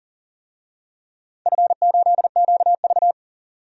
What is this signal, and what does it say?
Morse code, heard as a keyed tone, about 38 wpm, 700 Hz. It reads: F8QV
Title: DXCC-100 Call Signs - 4 Characters - 40wpm